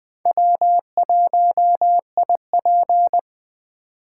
Morse code, heard as a keyed tone, about 20 wpm, 700 Hz. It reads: W1IP
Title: USA Call Signs - 4 Characters - 2x Character Spacing - 20wpm